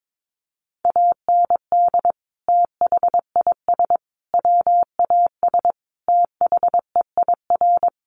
Morse code, heard as a keyed tone, about 22 wpm, 700 Hz. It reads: AND THIS WAS THEIR
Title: Sentences from Top 1100 Words 22wpm